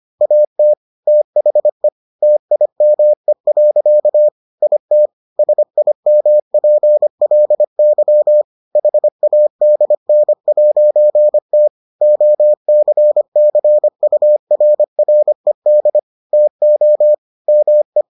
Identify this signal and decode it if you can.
Morse code, heard as a keyed tone, about 25 wpm, 600 Hz. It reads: AT THE TIME. IT SIMPLY HADN'T OCCURRED TO ME